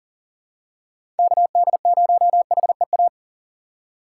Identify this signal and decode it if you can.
Morse code, heard as a keyed tone, about 39 wpm, 700 Hz. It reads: KD0HEA
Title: USA Call Signs - 6x Character Spacing - 40wpm